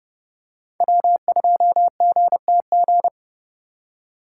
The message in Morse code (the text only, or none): W2GTG